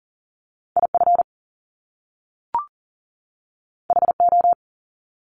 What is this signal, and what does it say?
Morse code, heard as a keyed tone, about 38 wpm, 700 Hz. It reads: IF HO
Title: Top Two Letter Combinations in Words 40wpm